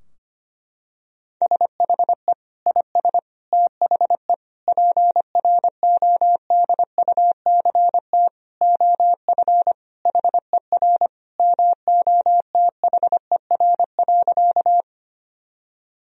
Morse code, heard as a keyed tone, about 25 wpm, 700 Hz. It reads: SHE IS THE PRODUCT OF HER MOTHER.
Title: Sentences from Top 1000 Words - Review Part 2 - 25wpm